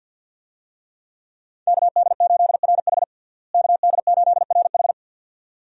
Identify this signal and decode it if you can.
Morse code, heard as a keyed tone, about 51 wpm, 700 Hz. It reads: KD8RH KD8RH
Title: USA Call Signs - 5 Characters - Repeated 2x - 50wpm